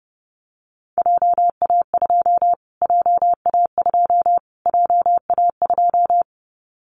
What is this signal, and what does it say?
Morse code, heard as a keyed tone, about 29 wpm, 700 Hz. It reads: JA2 JA2 JA2
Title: Callsign Prefixes for Top 25 DXCC - Repeated 3x - 30wpm